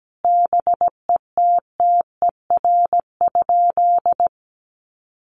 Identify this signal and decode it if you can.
Morse code, heard as a keyed tone, about 17 wpm, 700 Hz. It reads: BETTER?
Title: Sentences from Top 1500 Words 17wpm